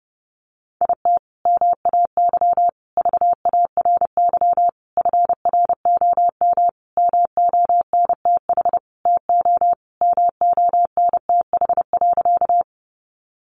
Morse code, heard as a keyed tone, about 30 wpm, 700 Hz. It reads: IT MAY VARY FROM MONTH TO MONTH.
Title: Sentences from Top 1200 Words 30wpm